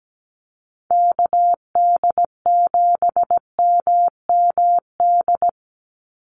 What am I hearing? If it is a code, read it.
Morse code, heard as a keyed tone, about 17 wpm, 700 Hz. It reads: KD7MMD
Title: USA Call Signs 17wpm